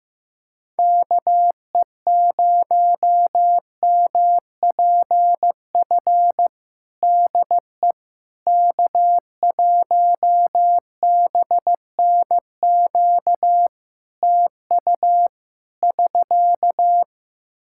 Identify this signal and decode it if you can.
Morse code, heard as a keyed tone, about 15 wpm, 700 Hz. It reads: KE0MPF DE K1BNQ TU <SK>
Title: Practice QSO - Exchange 4 - Ending - 15wpm